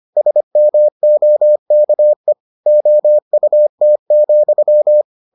Morse code, heard as a keyed tone, about 25 wpm, 600 Hz. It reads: SMOKE OUT,